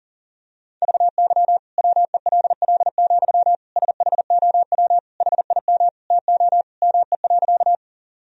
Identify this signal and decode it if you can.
Morse code, heard as a keyed tone, about 37 wpm, 700 Hz. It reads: VY WELL, SHOW HIM TO ME.
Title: Sentences from Top 200 Words and Abbreviations 40wpm